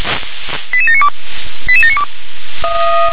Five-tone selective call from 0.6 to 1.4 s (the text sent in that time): 88792